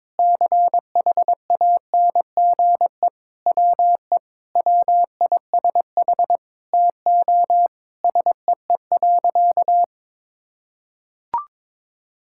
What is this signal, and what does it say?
Morse code, heard as a keyed tone, about 23 wpm, 700 Hz. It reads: CHANGE WE WISH TO SEE.
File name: Sentences with 7-Word Phrases - 22wpm